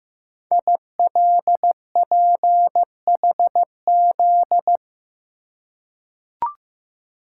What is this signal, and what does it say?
Morse code, heard as a keyed tone, about 15 wpm, 700 Hz. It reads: ILPHZ
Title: Code Groups - 5 Letter:Numbers - 15wpm